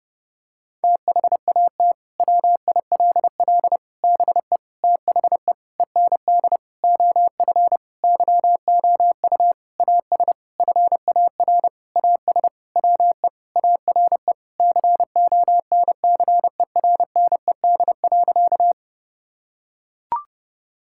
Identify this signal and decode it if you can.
Morse code, heard as a keyed tone, about 30 wpm, 700 Hz. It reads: THAT WILL BE THE END OF YOU AS FAR AS WE ARE CONCERNED.
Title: Sentences from Top 1000 Words - Review Part 6 - 30wpm